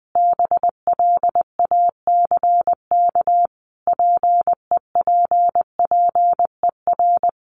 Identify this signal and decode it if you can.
Morse code, heard as a keyed tone, about 20 wpm, 700 Hz. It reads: BLACK PEPPER
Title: Sets of 5 Words Encore 20wpm